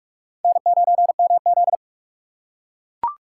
Morse code, heard as a keyed tone, about 41 wpm, 700 Hz. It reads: N9MZ
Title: USA Call Signs - 4 Characters - 3x Character Spacing - 45wpm